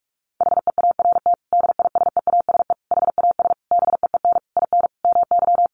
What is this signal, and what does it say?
Morse code, heard as a keyed tone, about 44 wpm, 700 Hz. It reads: HEART DISEASE HAS BEEN IN MY
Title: Sentences from Top 1200 Words 45wpm